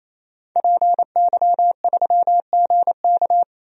Morse code, heard as a keyed tone, about 28 wpm, 700 Hz. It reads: PY3GK
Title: DXCC-100 Call Signs - 5 Characters - 28wpm